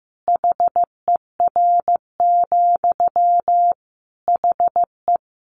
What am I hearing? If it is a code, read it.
Morse code, heard as a keyed tone, about 15 wpm, 700 Hz. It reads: HER, HE
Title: Sentences from Top 200 Words 15wpm